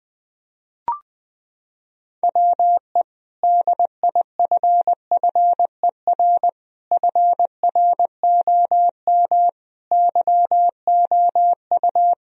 Morse code, heard as a keyed tone, about 20 wpm, 700 Hz. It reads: WE DIFFER FROM YOU
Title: Sentences from Top 1000 Words - Review Part 1 - 20wpm